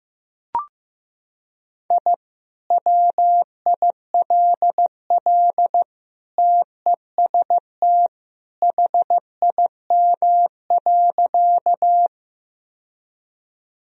Morse code, heard as a keyed tone, about 15 wpm, 700 Hz. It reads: I WILL TEST HIM.
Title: Sentences from Top 1000 Words - Review Part 3 - 15wpm